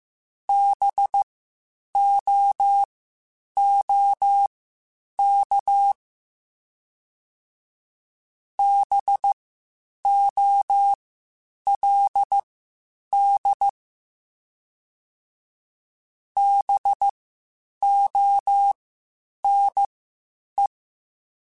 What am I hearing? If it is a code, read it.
Morse code, heard as a keyed tone, about 15 wpm, spaced as for 8 wpm, 800 Hz. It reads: BOOK BOLD BONE